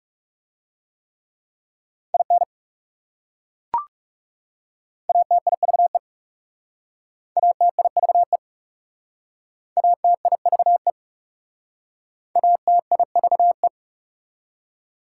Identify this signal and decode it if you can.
Morse code, heard as a keyed tone, about 40 wpm, 690 Hz. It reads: IN ATIVE ATIVE ATIVE ATIVE